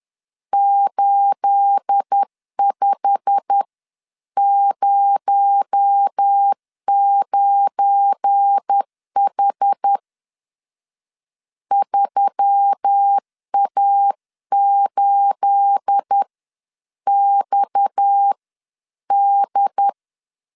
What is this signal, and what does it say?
Morse code, heard as a keyed tone, about 10 wpm, spaced as for 9 wpm, 800 Hz. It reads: 8509H 3A8XD